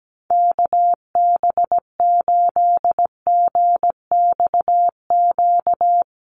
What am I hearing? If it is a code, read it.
Morse code, heard as a keyed tone, about 17 wpm, 700 Hz. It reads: KB8GXQ
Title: USA Call Signs Repeated 2x 17wpm